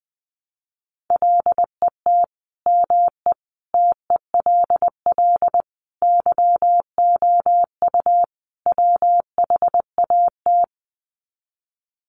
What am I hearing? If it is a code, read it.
Morse code, heard as a keyed tone, about 20 wpm, 700 Hz. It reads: LET ME TELL YOU WHAT
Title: Sets of 5 Words Encore 20wpm